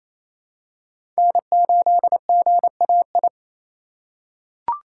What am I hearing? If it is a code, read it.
Morse code, heard as a keyed tone, about 27 wpm, 700 Hz. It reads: N8GAI
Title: USA Call Signs - 5 Characters - 2x Character Spacing - 28wpm